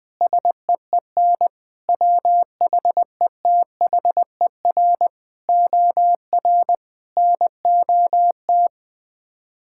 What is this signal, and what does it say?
Morse code, heard as a keyed tone, about 20 wpm, 700 Hz. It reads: SEEN WHETHER OR NOT
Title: Sets of 8 Words - 20wpm